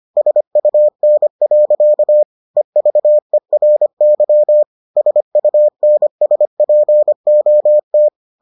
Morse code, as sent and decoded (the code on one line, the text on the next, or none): ... ..- -. .-.-.- / . ...- . .-. -.-- / ... ..- -. ... .--. --- -
SUN. EVERY SUNSPOT